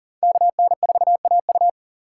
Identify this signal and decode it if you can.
Morse code, heard as a keyed tone, about 53 wpm, 700 Hz. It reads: KN4AU